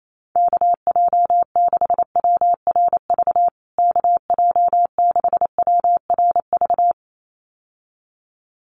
Morse code, heard as a keyed tone, about 28 wpm, 700 Hz. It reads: KJ6WRV KJ6WRV
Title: USA Call Signs Repeated 2x 28wpm